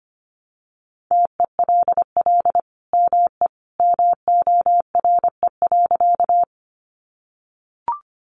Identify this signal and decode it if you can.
Morse code, heard as a keyed tone, about 25 wpm, 700 Hz. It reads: TELL ME MORE.
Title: Sentences from Top 300 Words 25wpm